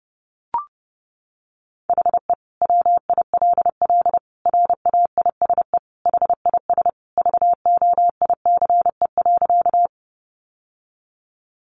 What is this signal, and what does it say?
Morse code, heard as a keyed tone, about 30 wpm, 700 Hz. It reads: HE WILL RAISE HIS VOICE.
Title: Sentences from Top 1400 Words 30wpm